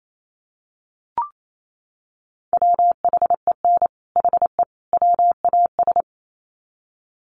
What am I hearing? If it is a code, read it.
Morse code, heard as a keyed tone, about 28 wpm, 700 Hz. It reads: WHEN HE WAS
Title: Sets of 3 Words 28wpm